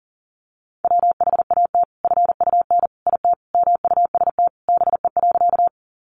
Morse code, heard as a keyed tone, about 37 wpm, 700 Hz. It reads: WHAT FUN IT MUST BE.
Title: Sentences from Top 700 Words 40wpm